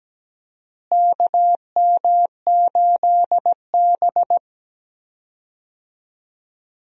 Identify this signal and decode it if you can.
Morse code, heard as a keyed tone, about 17 wpm, 700 Hz. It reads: KM8B
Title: USA Call Signs - 4 Characters - Repeated 2x - 17wpm